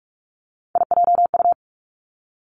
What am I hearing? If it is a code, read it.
Morse code, heard as a keyed tone, about 46 wpm, 700 Hz. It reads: IJU
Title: Three Challenging Letter-Number Combinations 45wpm